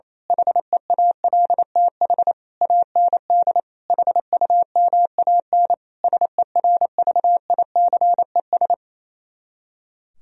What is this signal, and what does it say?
Morse code, heard as a keyed tone, about 26 wpm, 700 Hz. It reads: HEALTH AND HUMAN SERVICES
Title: Speed-Racing - Sets of 5 Words 28wpm